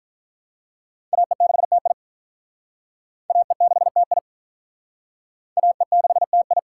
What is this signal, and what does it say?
Morse code, heard as a keyed tone, about 47 wpm, 700 Hz. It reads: AE6TI AE6TI AE6TI